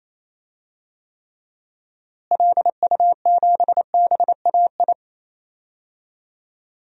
Morse code, heard as a keyed tone, about 28 wpm, 700 Hz. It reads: LU7BAI
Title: DXCC-100 Call Signs - 6 Characters - Repeated 2x - 28wpm